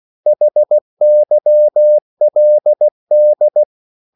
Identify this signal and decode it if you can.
Morse code, heard as a keyed tone, about 16 wpm, 600 Hz. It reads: HYLD
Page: Korte danske ord 16wpm